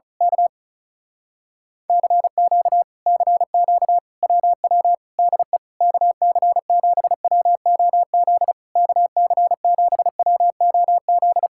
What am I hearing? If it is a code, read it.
Morse code, heard as a keyed tone, about 32 wpm, 700 Hz. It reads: K CQ CQ WW DE KC7WOZ KC7WOZ
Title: Speed-Racing Common QSO Phrases Encore - 35wpm